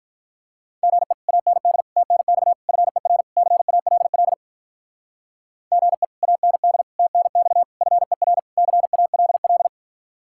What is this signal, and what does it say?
Morse code, heard as a keyed tone, about 45 wpm, 700 Hz. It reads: GE AND TNX FER CALL GE AND TNX FER CALL